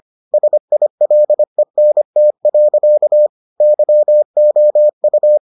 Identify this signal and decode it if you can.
Morse code, heard as a keyed tone, about 25 wpm, 600 Hz. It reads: SILENT. YOU